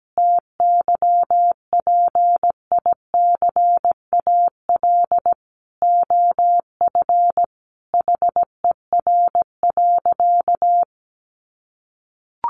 Morse code, heard as a keyed tone, about 17 wpm, 700 Hz. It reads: TYPICAL OF HER.
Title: Sentences from Top 1100 Words 17wpm